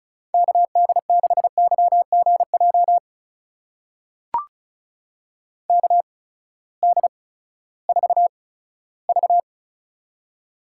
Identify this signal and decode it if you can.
Morse code, heard as a keyed tone, about 36 wpm, spaced as for 18 wpm, 700 Hz. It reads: KD6YGJ KD4V
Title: USA Call Signs - 8x Character Spacing - 35wpm